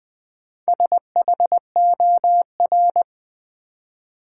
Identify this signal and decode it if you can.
Morse code, heard as a keyed tone, about 20 wpm, 700 Hz. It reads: SHOR
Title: Build A Word - 4 Letters At A Time - 8-Letter Words - 20wpm